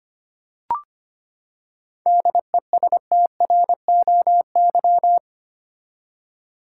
Morse code, heard as a keyed tone, about 25 wpm, 700 Hz. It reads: DESTROY
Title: Seven Letter Words - 25wpm